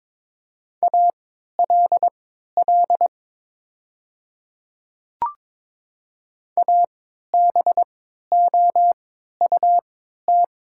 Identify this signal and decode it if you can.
Morse code, heard as a keyed tone, about 23 wpm, spaced as for 13 wpm, 700 Hz. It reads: ALL ABOUT
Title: 3x Character Spacing - Top 100 Words - 22wpm